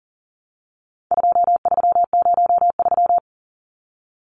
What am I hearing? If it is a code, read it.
Morse code, heard as a keyed tone, about 40 wpm, 700 Hz. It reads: 2303
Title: Numbers - 4-Digit 40wpm